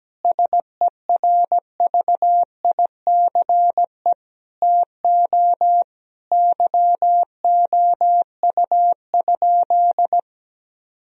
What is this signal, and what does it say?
Morse code, heard as a keyed tone, about 17 wpm, 700 Hz. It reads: SERVICE TO YOU?
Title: Sentences from Top 1000 Words - Review Part 4 - 17wpm